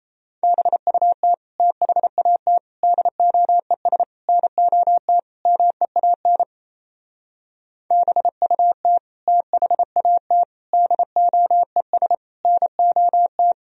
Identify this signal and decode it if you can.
Morse code, heard as a keyed tone, about 35 wpm, 700 Hz. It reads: BUT THAT DOES NOT MEAN BUT THAT DOES NOT
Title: Speed-Racing - Sets of 5 Words 28wpm